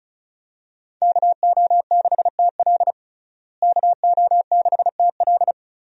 Morse code, heard as a keyed tone, about 36 wpm, 700 Hz. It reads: KO6TL KO6TL
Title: USA Call Signs - 5 Characters - Repeated 2x - 35wpm